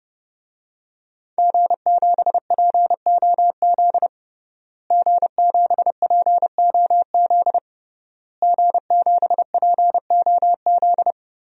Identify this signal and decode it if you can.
Morse code, heard as a keyed tone, about 32 wpm, 700 Hz. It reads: G7POZ G7POZ G7POZ
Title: DXCC-100 Call Signs - 5 Characters - Repeated 3x - 30wpm